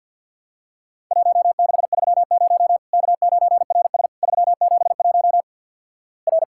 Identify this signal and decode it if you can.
Morse code, heard as a keyed tone, about 50 wpm, 700 Hz. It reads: 1630 K9RS 371 R
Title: Worked All Europe DX Contest - QTC Exchange with Repeats - 50wpm